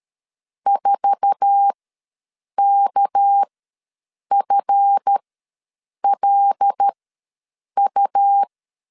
Morse code, heard as a keyed tone, about 13 wpm, spaced as for 7 wpm, 800 Hz. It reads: 4KFLU
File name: LEZIONI / 8 YJ45 / 8 YJ4 ws 12wpm 3249